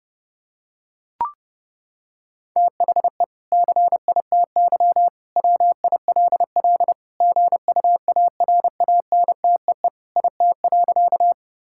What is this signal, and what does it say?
Morse code, heard as a keyed tone, about 30 wpm, 700 Hz. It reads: THE CITY WILL GUARANTEE IT.
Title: Sentences from Top 1600 Words 30wpm